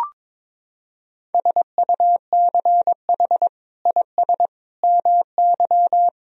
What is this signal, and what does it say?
Morse code, heard as a keyed tone, about 22 wpm, 700 Hz. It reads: SUCH IS MY
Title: Sentences from Top 1000 Words - Review Part 1 - 22wpm